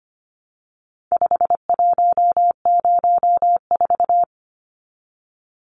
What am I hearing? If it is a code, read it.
Morse code, heard as a keyed tone, about 25 wpm, 700 Hz. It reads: 5104
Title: Numbers - 4-Digit 25wpm